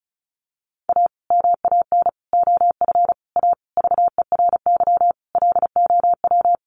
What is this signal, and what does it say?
Morse code, heard as a keyed tone, about 34 wpm, 700 Hz. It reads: A MAN OF A VERY LOW
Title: Sentences from Top 1200 Words 35wpm